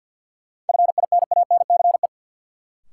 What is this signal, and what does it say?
Morse code, heard as a keyed tone, about 47 wpm, 700 Hz. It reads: FINANCE
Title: Sentences from Top 1500 Words - New Word Drill - 50wpm